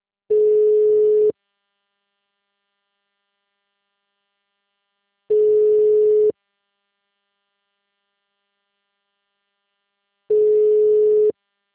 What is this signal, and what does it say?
Ringback tone (Europe, 425 Hz).